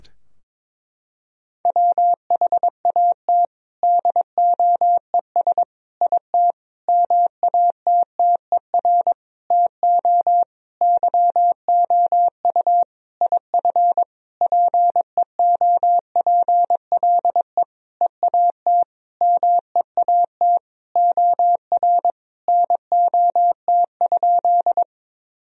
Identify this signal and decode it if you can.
Morse code, heard as a keyed tone, about 22 wpm, 700 Hz. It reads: WHAT DOES IT MATTER TO YOU IF PEOPLE EAT MEAT OR NOT?
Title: Sentences from Top 1000 Words - Review Part 3 - 22wpm